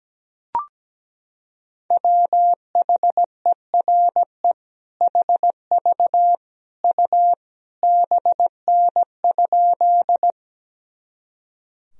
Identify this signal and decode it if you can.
Morse code, heard as a keyed tone, about 17 wpm, 700 Hz. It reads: WHERE HV U BN?
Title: Sentences from Top 200 Words and Abbreviations 17wpm